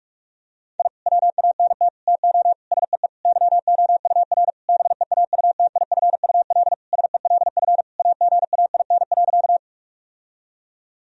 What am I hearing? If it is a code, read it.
Morse code, heard as a keyed tone, about 46 wpm, 700 Hz. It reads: I WANT TO SEE YOUR BEAUTIFUL SELF AGAIN.
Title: Sentences from Top 1000 Words - Review Part 1 - 45wpm